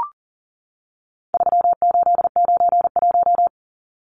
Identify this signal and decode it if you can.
Morse code, heard as a keyed tone, about 40 wpm, 700 Hz. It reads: 3891
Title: Numbers - 4-Digit 40wpm